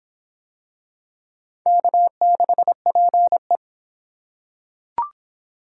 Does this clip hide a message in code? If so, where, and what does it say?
Yes: Morse code from 1.7 to 3.6 s, heard as a keyed tone, about 26 wpm, 700 Hz. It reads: K6PE